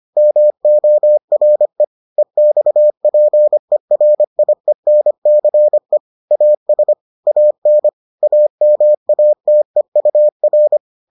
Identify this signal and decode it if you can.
Morse code, heard as a keyed tone, about 25 wpm, 600 Hz. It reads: MORE EXPERIENCE AS AN AMATEUR